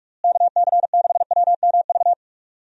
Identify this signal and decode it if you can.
Morse code, heard as a keyed tone, about 46 wpm, 700 Hz. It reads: KC6WMV
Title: USA Call Signs - 6 Characters - Repeated 3x - 45wpm